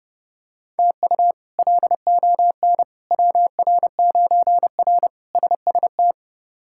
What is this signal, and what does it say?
Morse code, heard as a keyed tone, about 30 wpm, 700 Hz. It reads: TU LON WR9R SST
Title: K1USN SST QSO Practice - 30wpm